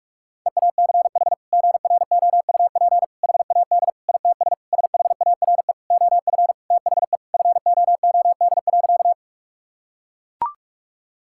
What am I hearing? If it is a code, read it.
Morse code, heard as a keyed tone, about 41 wpm, 690 Hz. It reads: EACH GROUP HAD ITS SHARE OF THE FOOD.
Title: Sentences from Top 1000 Words - Review Part 6 - 45wpm